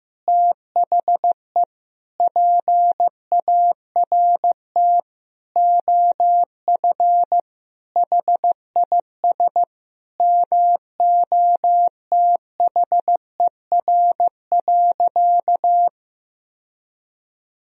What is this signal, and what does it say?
Morse code, heard as a keyed tone, about 15 wpm, 700 Hz. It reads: THE PART OF HIS MOTHER.
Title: Sentences from Top 1000 Words - Review Part 4 - 15wpm